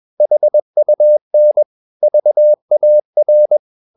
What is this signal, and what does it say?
Morse code, heard as a keyed tone, about 21 wpm, 600 Hz. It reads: HUN VAR
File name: Ferie QRS 21wpm